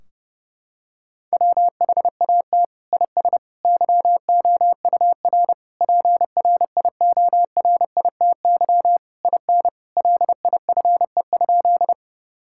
Morse code, heard as a keyed tone, about 30 wpm, 700 Hz. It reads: WHAT IS YOUR PRIORITY IN LIFE?
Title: Sentences from Top 1600 Words 30wpm